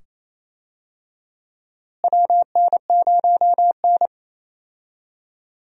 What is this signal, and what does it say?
Morse code, heard as a keyed tone, about 27 wpm, 700 Hz. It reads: WN0N
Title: USA Call Signs - 4 Characters - Repeated 2x - 28wpm